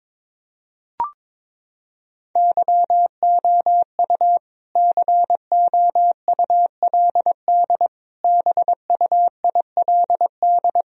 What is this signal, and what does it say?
Morse code, heard as a keyed tone, about 22 wpm, 700 Hz. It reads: YOU COULD BUILD
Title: Sentences from Top 1000 Words - Review Part 2 - 22wpm